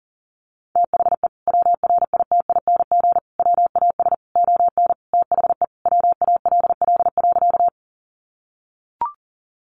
Morse code, heard as a keyed tone, about 40 wpm, 700 Hz. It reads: THE WRITING WAS ON THE WALL.
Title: Sentences from Top 1100 Words 40wpm